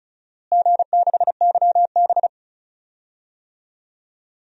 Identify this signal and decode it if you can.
Morse code, heard as a keyed tone, about 34 wpm, 700 Hz. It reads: G6YB
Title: DXCC-100 Call Signs - 4 Characters - Repeated 3x - 35wpm